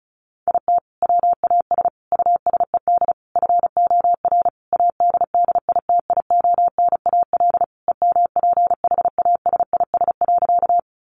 Morse code, heard as a keyed tone, about 34 wpm, 700 Hz. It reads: IT WAS USED FOR ADDITIONAL EMPHASIS.
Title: Sentences from Top 1400 Words 35wpm